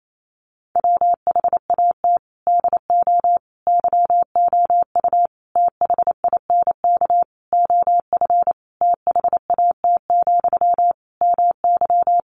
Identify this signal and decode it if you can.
Morse code, heard as a keyed tone, about 28 wpm, 700 Hz. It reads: WHAT DO YOU THINK OF THAT, MY
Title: Sentences from Top 1500 Words 28wpm